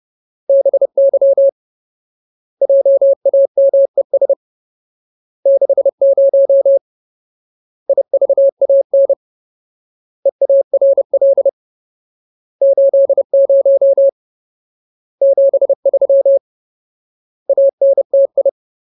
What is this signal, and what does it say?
Morse code, heard as a keyed tone, about 30 wpm, 550 Hz. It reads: BY JAMES 60 IVAN EARL 80 73 ANTI
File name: Ultimate Word List / 30WPM Ultimate